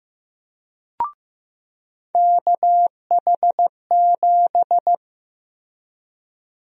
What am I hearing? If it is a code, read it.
Morse code, heard as a keyed tone, about 15 wpm, 700 Hz. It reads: KH7
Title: USA Call Sign Prefixes - Repeated 2x - 15wpm